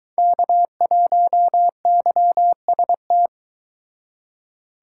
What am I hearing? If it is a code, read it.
Morse code, heard as a keyed tone, about 22 wpm, 700 Hz. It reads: K1YST